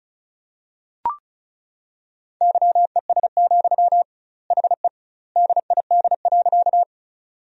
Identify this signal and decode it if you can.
Morse code, heard as a keyed tone, about 34 wpm, 700 Hz. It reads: YES, HE DID.
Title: 2x Word-Spacing - Sentences from Top 500 Words - 35wpm